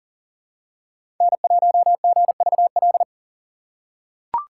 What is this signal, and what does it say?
Morse code, heard as a keyed tone, about 40 wpm, 700 Hz. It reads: N1GVL
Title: USA Call Signs - 5 Characters - 2x Character Spacing - 40wpm